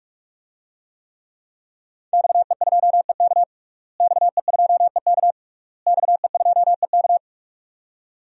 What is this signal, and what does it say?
Morse code, heard as a keyed tone, about 46 wpm, 700 Hz. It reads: XE2EK XE2EK XE2EK
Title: DXCC-100 Call Signs - 5 Characters - Repeated 3x - 45wpm